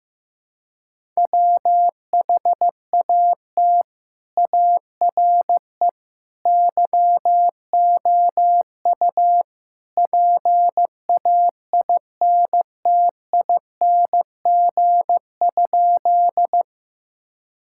Morse code, heard as a keyed tone, about 15 wpm, 700 Hz. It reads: WHAT ARE YOU PAINTING?
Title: Sentences from Top 1000 Words - Review Part 6 - 15wpm